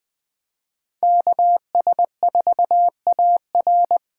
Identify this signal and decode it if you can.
Morse code, heard as a keyed tone, about 23 wpm, 700 Hz. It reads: KS4AR
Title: Speed Racing - USA 2x2 Call Signs 20wpm